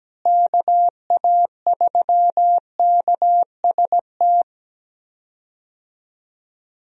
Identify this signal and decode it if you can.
Morse code, heard as a keyed tone, about 17 wpm, 700 Hz. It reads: KA3KST
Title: USA Call Signs - 6 Characters - Repeated 3x - 17wpm